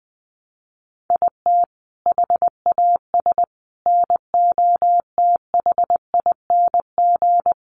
Morse code, heard as a keyed tone, about 20 wpm, 700 Hz. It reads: IT HAS NOTHING
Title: Sets of 5 Words Encore 20wpm